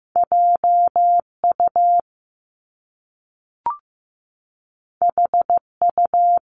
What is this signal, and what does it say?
Morse code, heard as a keyed tone, about 15 wpm, 700 Hz. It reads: JU HU
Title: Top Two Letter Combinations in Words 15wpm